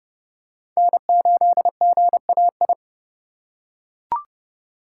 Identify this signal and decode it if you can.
Morse code, heard as a keyed tone, about 29 wpm, 700 Hz. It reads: N8GAI
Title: USA Call Signs - 5 Characters - 6x Character Spacing - 30wpm